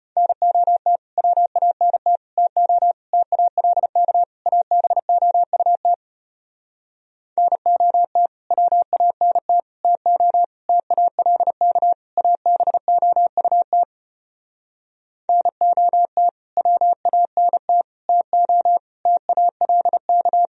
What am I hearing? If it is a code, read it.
Morse code, heard as a keyed tone, about 33 wpm, 700 Hz. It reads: NOT WANT TO TALK ABOUT NOT WANT TO TALK ABOUT NOT WANT TO TALK